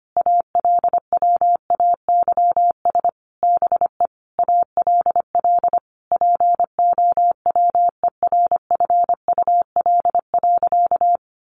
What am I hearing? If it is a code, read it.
Morse code, heard as a keyed tone, about 25 wpm, 700 Hz. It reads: ALWAYS BE ALL POWERFUL.
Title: Sentences from Top 1300 Words 25wpm